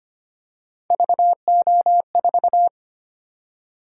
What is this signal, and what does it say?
Morse code, heard as a keyed tone, about 25 wpm, 700 Hz. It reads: VO4
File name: Three Letter-Number Combinations - 25wpm